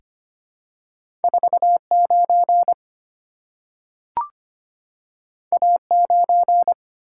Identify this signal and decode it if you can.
Morse code, heard as a keyed tone, about 25 wpm, 700 Hz. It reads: 49 A9
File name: Two Letter-Number Combinations - 25wpm